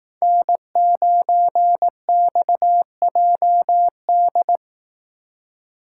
Morse code, heard as a keyed tone, about 18 wpm, 700 Hz. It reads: N9XJD
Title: Speed Racing - USA 1x3 Call Signs 15wpm